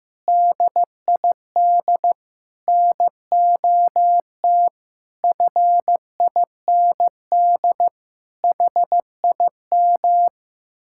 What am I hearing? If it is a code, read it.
Morse code, heard as a keyed tone, about 15 wpm, 700 Hz. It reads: DID NOT FIND HIM
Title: Sentences from Top 1600 Words 15wpm